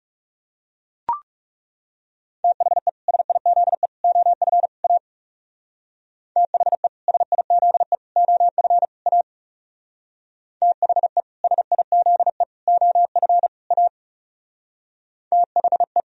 Morse code, heard as a keyed tone, about 36 wpm, 700 Hz. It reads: THE SIZE OF A THE SIZE OF A THE SIZE OF A THE
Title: Speed-Racing - Sets of 4 Words 30wpm